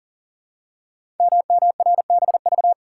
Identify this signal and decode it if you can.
Morse code, heard as a keyed tone, about 43 wpm, 700 Hz. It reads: MMRBV
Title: Code Groups - 5 Letters - 40wpm